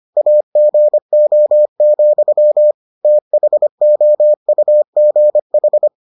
Morse code, heard as a keyed tone, about 25 wpm, 600 Hz. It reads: AGO, THOUGH